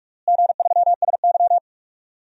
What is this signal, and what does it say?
Morse code, heard as a keyed tone, about 44 wpm, 700 Hz. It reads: G3SY
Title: DXCC-100 Call Signs - 4 Characters - Repeated 3x - 45wpm